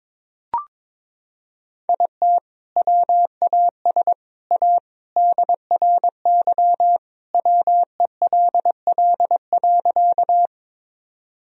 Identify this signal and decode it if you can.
Morse code, heard as a keyed tone, about 22 wpm, 700 Hz. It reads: IT WAS A DRY WELL.
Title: Sentences from Top 1000 Words - Review Part 3 - 22wpm